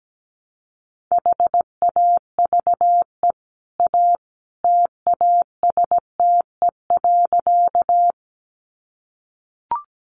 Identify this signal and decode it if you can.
Morse code, heard as a keyed tone, about 17 wpm, 700 Hz. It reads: HAVE A TASTE.
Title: Sentences from Top 1100 Words 17wpm